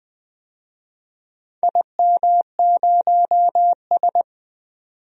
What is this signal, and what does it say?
Morse code, heard as a keyed tone, about 20 wpm, 700 Hz. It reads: IM0S
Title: DXCC-100 Call Signs - 4 Characters - 20wpm